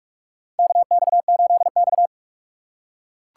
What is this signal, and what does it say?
Morse code, heard as a keyed tone, about 42 wpm, 700 Hz. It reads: KX8X
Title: K1USN SST QSO Practice - 45wpm